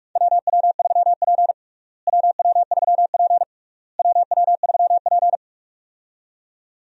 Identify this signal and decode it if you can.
Morse code, heard as a keyed tone, about 46 wpm, 700 Hz. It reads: WW3P WW3P WW3P
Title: USA Call Signs - 4 Characters - Repeated 3x - 45wpm